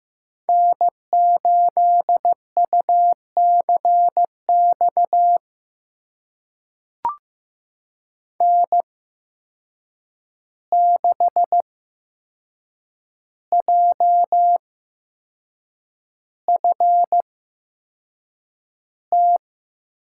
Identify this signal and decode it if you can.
Morse code, heard as a keyed tone, about 15 wpm, spaced as for 6 wpm, 700 Hz. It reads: N8UCX N6JFT
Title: USA Call Signs - 5 Characters - 8x Character Spacing - 15wpm